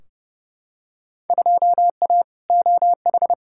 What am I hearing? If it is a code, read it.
Morse code, heard as a keyed tone, about 30 wpm, 700 Hz. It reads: 2A OH
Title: Field Day Exchanges - 30wpm